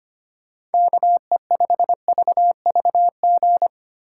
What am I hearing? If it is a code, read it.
Morse code, heard as a keyed tone, about 32 wpm, 700 Hz. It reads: KE5VVG